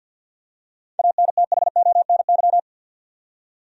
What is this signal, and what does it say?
Morse code, heard as a keyed tone, about 53 wpm, 700 Hz. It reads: ANTHONY
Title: USA Common Names - 50wpm